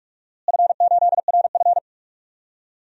Morse code, heard as a keyed tone, about 46 wpm, 700 Hz. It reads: F8RF
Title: DXCC-100 Call Signs - 4 Characters - Repeated 2x - 45wpm